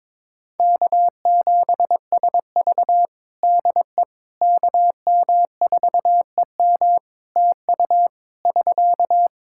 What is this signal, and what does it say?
Morse code, heard as a keyed tone, about 22 wpm, 700 Hz. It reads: K7SV DE KM4EM TU <SK>
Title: Practice QSO - Exchange 4 - Ending - 22wpm